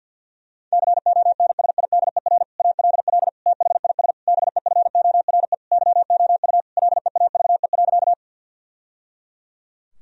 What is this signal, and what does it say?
Morse code, heard as a keyed tone, about 51 wpm, 700 Hz. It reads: CONSIDER ALL THIS BEFORE YOU LEAVE.
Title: Sentences from Top 1000 Words - Review Part 4 - 50wpm